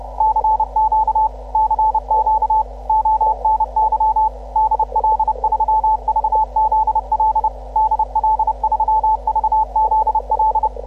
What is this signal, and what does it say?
Morse code, heard as a keyed tone, about 30 wpm, 870 Hz, with radio noise. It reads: CQ CQ ONY DL3VZL DL3VZL